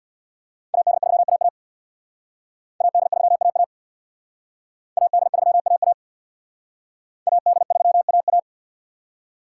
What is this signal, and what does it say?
Morse code, heard as a keyed tone, about 59 wpm, 700 Hz. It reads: AD3AA AD3AA AD3AA AD3AA
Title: Speed Racing - USA 2x2 Call Signs 50wpm